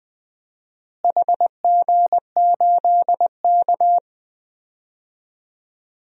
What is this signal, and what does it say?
Morse code, heard as a keyed tone, about 20 wpm, 700 Hz. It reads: HG8K
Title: DXCC-100 Call Signs - 4 Characters - Repeated 3x - 20wpm